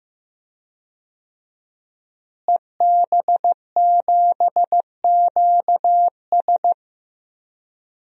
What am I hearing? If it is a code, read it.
Morse code, heard as a keyed tone, about 15 wpm, 700 Hz. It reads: EB7QS
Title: DXCC-100 Call Signs - 5 Characters - Repeated 3x - 15wpm